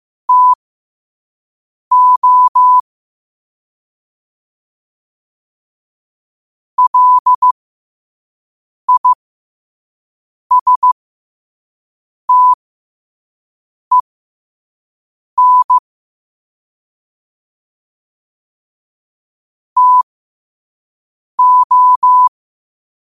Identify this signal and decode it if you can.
Morse code, heard as a keyed tone, about 15 wpm, spaced as for 5 wpm, 1000 Hz. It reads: TO LISTEN TO
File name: qotdfiles / 5wpm